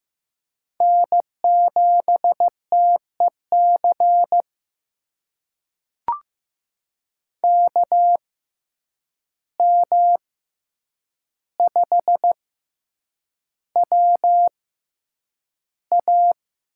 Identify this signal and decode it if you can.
Morse code, heard as a keyed tone, about 15 wpm, spaced as for 8 wpm, 700 Hz. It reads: N7TEC KM5WA
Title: USA Call Signs - 5 Characters - 6x Character Spacing - 15wpm